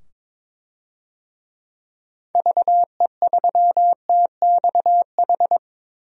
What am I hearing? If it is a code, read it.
Morse code, heard as a keyed tone, about 22 wpm, 700 Hz. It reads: VE3TXH
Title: DXCC-100 Call Signs - 6 Characters - 22wpm